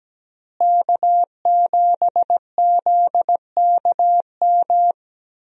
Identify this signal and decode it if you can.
Morse code, heard as a keyed tone, about 17 wpm, 700 Hz. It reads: K7ZKM
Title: USA Call Signs - 5 Characters - Repeated 2x - 17wpm